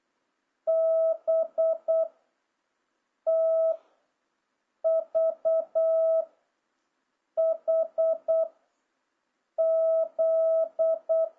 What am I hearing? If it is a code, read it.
Morse code, heard as a keyed tone, about 8 wpm, 650 Hz. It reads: B T V H Z
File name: jh8cbh / 2026cw